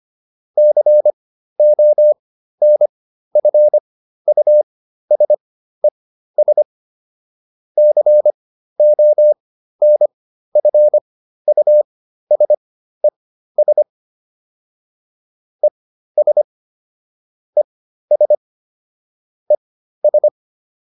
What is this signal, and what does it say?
Morse code, heard as a keyed tone, about 25 wpm, spaced as for 13 wpm, 600 Hz. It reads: CONFUSES CONFUSES ES ES ES